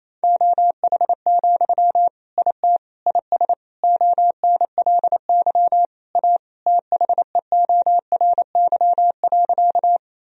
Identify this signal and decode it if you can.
Morse code, heard as a keyed tone, about 28 wpm, 700 Hz. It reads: OH, IT IS ONLY A THEORY.
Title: Sentences from Top 1000 Words - Review Part 3 - 28wpm